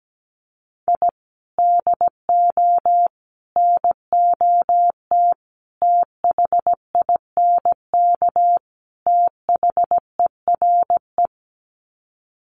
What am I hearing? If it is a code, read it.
Morse code, heard as a keyed tone, about 17 wpm, 700 Hz. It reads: I DO NOT THINK THERE
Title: Sets of 5 Words 17wpm